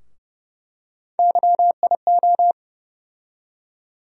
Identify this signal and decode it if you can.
Morse code, heard as a keyed tone, about 30 wpm, 700 Hz. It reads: YIO
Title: USA Call Sign Postfixes - 3 Letters 30wpm